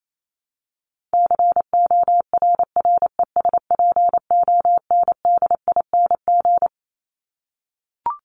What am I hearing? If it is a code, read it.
Morse code, heard as a keyed tone, about 28 wpm, 700 Hz. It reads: CORRESPONDING
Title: ING Suffix - Top 100 Words with Base Words 28wpm